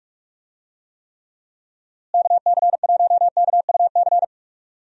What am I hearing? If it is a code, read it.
Morse code, heard as a keyed tone, about 46 wpm, 700 Hz. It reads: KC1KUC
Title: USA Call Signs - 6 Characters - Repeated 3x - 45wpm